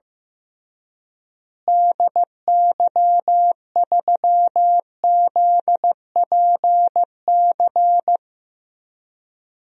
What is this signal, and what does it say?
Morse code, heard as a keyed tone, about 15 wpm, 700 Hz. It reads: DY3ZPC
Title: DXCC-100 Call Signs - 6 Characters - Repeated 3x - 15wpm